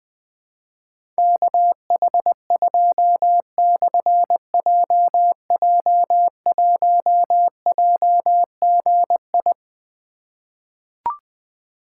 Morse code, heard as a keyed tone, about 20 wpm, 700 Hz. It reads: KH2/JJ1JGI
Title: Difficult Call Signs 20wpm